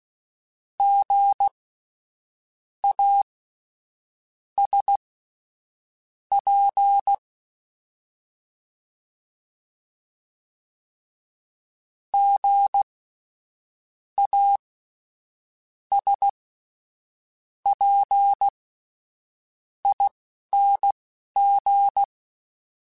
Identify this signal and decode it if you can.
Morse code, heard as a keyed tone, about 16 wpm, spaced as for 6 wpm, 800 Hz. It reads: GASP GASPING
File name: Spaced Alphas / Spaced Alphas mp3s / Gs sp @10